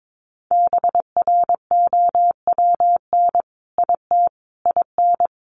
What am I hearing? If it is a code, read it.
Morse code, heard as a keyed tone, about 22 wpm, 700 Hz. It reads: BROWN IT IN
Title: Sentences from Top 1200 Words 22wpm